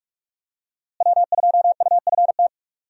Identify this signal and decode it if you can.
Morse code, heard as a keyed tone, about 48 wpm, 700 Hz. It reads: W2UFT
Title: USA Call Signs - 5 Characters - Repeated 3x - 45wpm